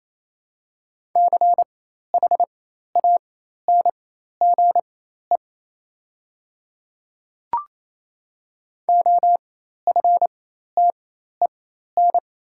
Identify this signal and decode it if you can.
Morse code, heard as a keyed tone, about 27 wpm, spaced as for 13 wpm, 690 Hz. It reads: CHANGE OFTEN
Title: 4x Character Spacing - Top 300 Words - 28wpm